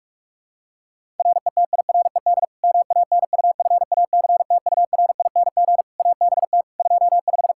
Morse code, heard as a keyed tone, about 46 wpm, 700 Hz. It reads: RETIRED MANUFACTURING ABT 25
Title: K1USN SST QSO Practice 45wpm